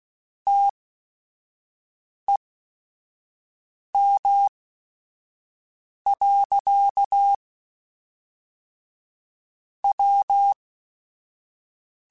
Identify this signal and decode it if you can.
Morse code, heard as a keyed tone, about 16 wpm, spaced as for 5 wpm, 800 Hz. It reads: TEM. W
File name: Technical Subjects / Technical Subjects-mp3s / Sw Pwr Sup @5